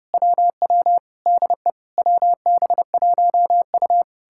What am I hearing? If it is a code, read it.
Morse code, heard as a keyed tone, about 33 wpm, 700 Hz. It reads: WW DE WB1U